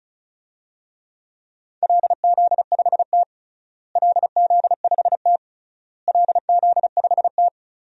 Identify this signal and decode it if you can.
Morse code, heard as a keyed tone, about 34 wpm, 700 Hz. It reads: LZ5T LZ5T LZ5T
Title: DXCC-100 Call Signs - 4 Characters - Repeated 3x - 35wpm